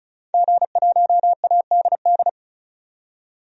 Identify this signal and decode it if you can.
Morse code, heard as a keyed tone, about 34 wpm, 700 Hz. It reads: G1ADD
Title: DXCC-100 Call Signs - 5 Characters - 35wpm